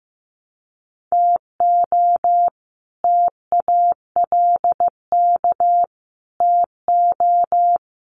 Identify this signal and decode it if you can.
Morse code, heard as a keyed tone, about 15 wpm, 700 Hz. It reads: TO TALK TO
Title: Sets of 5 Words Encore 15wpm